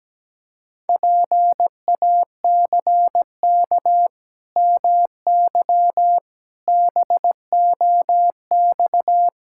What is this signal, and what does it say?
Morse code, heard as a keyed tone, about 17 wpm, 700 Hz. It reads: PACK MY BOX
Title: Pangram Challenge v2 - 17wpm